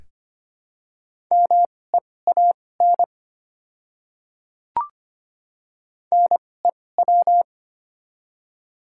Morse code, heard as a keyed tone, about 25 wpm, spaced as for 18 wpm, 700 Hz. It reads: MEAN NEW
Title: 2x Character Spacing - Top 200 Words - 25wpm